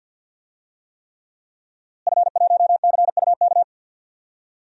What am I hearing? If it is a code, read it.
Morse code, heard as a keyed tone, about 53 wpm, 700 Hz. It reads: F1CUK
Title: DXCC-100 Call Signs - 5 Characters - Repeated 3x - 50wpm